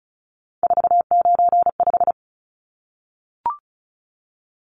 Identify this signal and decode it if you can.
Morse code, heard as a keyed tone, about 34 wpm, 700 Hz. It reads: 495
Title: Numbers - 3-Digit 35wpm